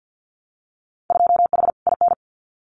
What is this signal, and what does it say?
Morse code, heard as a keyed tone, about 50 wpm, 700 Hz. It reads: 2H IN
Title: Winter Field Day Exchanges Encore 50wpm